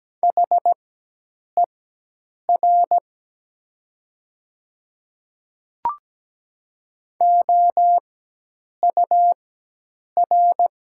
Morse code, heard as a keyed tone, about 17 wpm, spaced as for 8 wpm, 700 Hz. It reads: HER OUR
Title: 4x Character Spacing - Top 200 Words - 17wpm